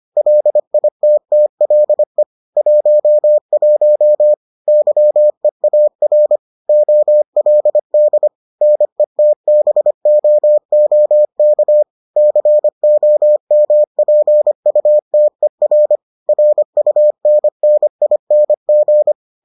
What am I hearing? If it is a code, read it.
Morse code, heard as a keyed tone, about 25 wpm, 600 Hz. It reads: LITTLE 11 YEAR OLD NETBOOK COMPUTER RUNNING